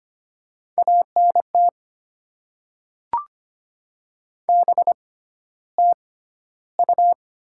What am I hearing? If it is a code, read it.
Morse code, heard as a keyed tone, about 25 wpm, spaced as for 13 wpm, 700 Hz. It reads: ANT BTU
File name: Top 20 QSO Elements - 6x Character Spacing - 25wpm